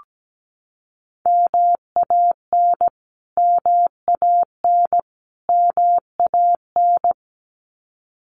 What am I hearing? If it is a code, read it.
Morse code, heard as a keyed tone, about 17 wpm, 700 Hz. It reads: MAN MAN MAN
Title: Top 200 Words Repeated 17wpm